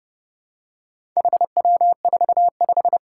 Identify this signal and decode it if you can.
Morse code, heard as a keyed tone, about 30 wpm, 700 Hz. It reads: HW45
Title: Four Challenging Letter-Number Combinations 30wpm